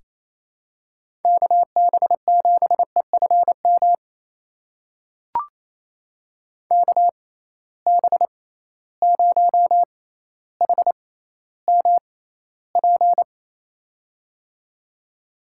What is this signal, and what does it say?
Morse code, heard as a keyed tone, about 28 wpm, spaced as for 14 wpm, 700 Hz. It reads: KB7EFM KB0HMP
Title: USA Call Signs - 6 Characters - 6x Character Spacing - 28wpm